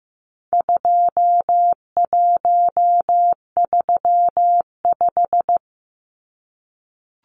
Morse code, heard as a keyed tone, about 15 wpm, 700 Hz. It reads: 2135
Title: Numbers - 4-Digit 15wpm